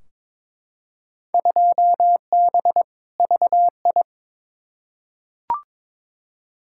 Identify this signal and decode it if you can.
Morse code, heard as a keyed tone, about 22 wpm, 700 Hz. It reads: 2B VI
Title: Field Day Exchanges Encore - 22wpm